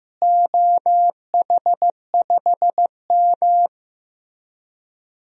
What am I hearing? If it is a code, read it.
Morse code, heard as a keyed tone, about 15 wpm, 700 Hz. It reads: OH5M
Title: DXCC-100 Call Signs - 4 Characters - Repeated 3x - 15wpm